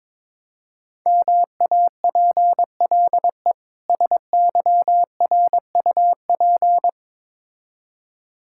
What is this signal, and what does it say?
Morse code, heard as a keyed tone, about 22 wpm, 700 Hz. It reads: MAPLE SYRUP
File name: Difficult Words with Hints - 22wpm